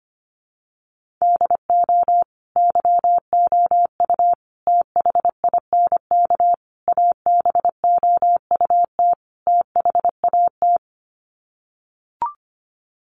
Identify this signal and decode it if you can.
Morse code, heard as a keyed tone, about 25 wpm, 700 Hz. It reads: DO YOU THINK ABOUT THAT
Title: Sets of 5 Words Encore2 25wpm